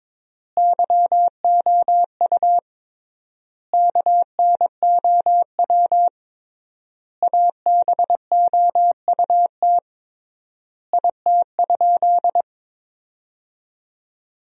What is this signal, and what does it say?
Morse code, heard as a keyed tone, about 23 wpm, 700 Hz. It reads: YOU KNOW ABOUT IT?
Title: 3x Word-Spacing - Sentences from Top 100 Words - 22wpm